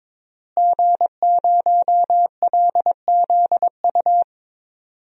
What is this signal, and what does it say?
Morse code, heard as a keyed tone, about 23 wpm, 700 Hz. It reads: G0LZU
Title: DXCC-100 Call Signs - 5 Characters - Repeated 3x - 22wpm